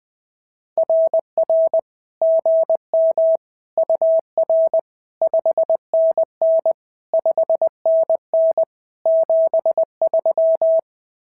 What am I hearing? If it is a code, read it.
Morse code, heard as a keyed tone, about 20 wpm, 650 Hz. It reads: RR GM UR 5NN 5NN 73
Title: SOTA QSO Practice 20wpm